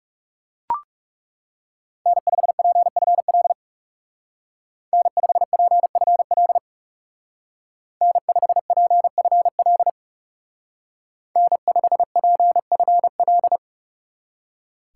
Morse code, heard as a keyed tone, about 36 wpm, 700 Hz. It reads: N5PFL N5PFL N5PFL N5PFL
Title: Speed Racing - USA 1x3 Call Signs 30wpm